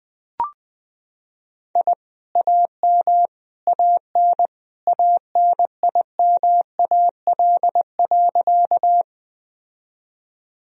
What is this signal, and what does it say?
Morse code, heard as a keyed tone, about 21 wpm, 700 Hz. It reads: I AM AN ANIMAL.
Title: Sentences from Top 1000 Words - Review Part 1 - 20wpm